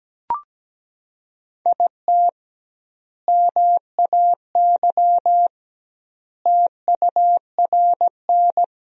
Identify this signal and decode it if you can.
Morse code, heard as a keyed tone, about 17 wpm, 700 Hz. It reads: IT MAY TURN
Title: 2x Word-Spacing - Sentences from Top 400 Words - 17wpm